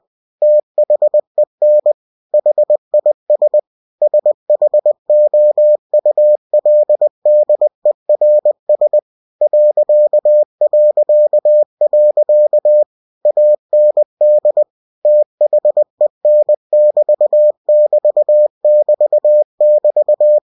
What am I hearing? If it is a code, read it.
Morse code, heard as a keyed tone, about 20 wpm, 600 Hz. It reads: THEN HIS SHOULDERS ... AND THEN<BT><BT><BT><BT>
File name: cw-books / winnie-the-pooh / winnie-20wpm-page-0016